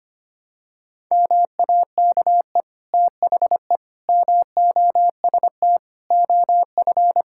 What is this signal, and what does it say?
Morse code, heard as a keyed tone, about 25 wpm, 700 Hz. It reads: MAKE THE MOST OF
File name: practice / 25wpm / Difficult Sentences with Hints - 25wpm